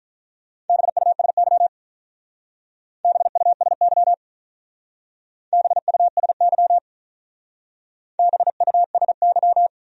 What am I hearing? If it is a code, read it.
Morse code, heard as a keyed tone, about 43 wpm, 700 Hz. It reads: BUSY BUSY BUSY BUSY
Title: Speed-Racing Top 500 Words 35wpm